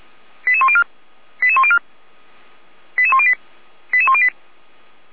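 Five-tone selective call (ZVEI), heard as quick runs of five tones, tones of about 70 ms. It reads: 88104 88104 88108 88108
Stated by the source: ZVEI five-tone selective call received by SDR, from fire brigade Feuerwehr Orlamünde.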